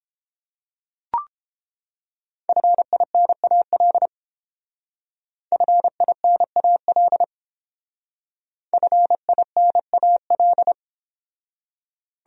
Morse code, heard as a keyed tone, about 30 wpm, 700 Hz. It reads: FINAL FINAL FINAL